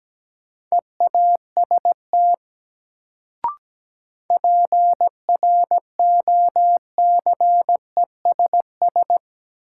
Morse code, heard as a keyed tone, about 17 wpm, 700 Hz. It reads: EAST PROCESS
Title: Sentences from Top 1000 Words - Review Part 3 - New Word Drill - 17wpm